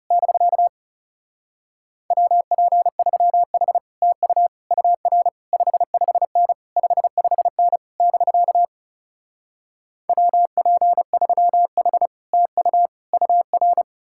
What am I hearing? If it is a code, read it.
Morse code, heard as a keyed tone, about 38 wpm, 700 Hz. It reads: <BK> WP3H TU UR 55N 55N <BK> WP3H TU UR
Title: Speed-Racing Common QSO Phrases - 30wpm